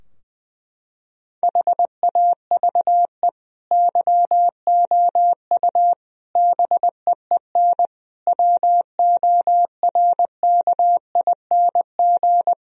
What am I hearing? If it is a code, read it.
Morse code, heard as a keyed tone, about 20 wpm, 700 Hz. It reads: HAVE YOU BEEN WORKING
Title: Sentences from Top 1000 Words - Review Part 4 - 20wpm